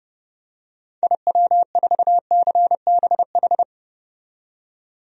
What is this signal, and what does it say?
Morse code, heard as a keyed tone, about 31 wpm, 700 Hz. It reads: IW4CBH
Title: DXCC-100 Call Signs - 6 Characters - 30wpm